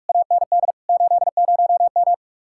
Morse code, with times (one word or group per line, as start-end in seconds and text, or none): 0.1-0.7 AND
0.9-2.1 80M